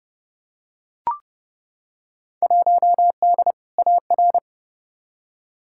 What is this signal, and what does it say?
Morse code, heard as a keyed tone, about 31 wpm, 700 Hz. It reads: 1D AR
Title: Field Day Exchanges - 30wpm